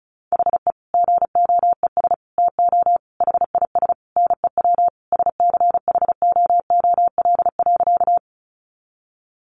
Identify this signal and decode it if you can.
Morse code, heard as a keyed tone, about 34 wpm, 700 Hz. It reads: HE GOES TO HIS NEW SCHOOL.
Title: Easy Phrases 35wpm